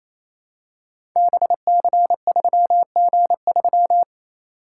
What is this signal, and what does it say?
Morse code, heard as a keyed tone, about 27 wpm, 700 Hz. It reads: BC3G3
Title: Code Groups - 5 Letter:Numbers - 28wpm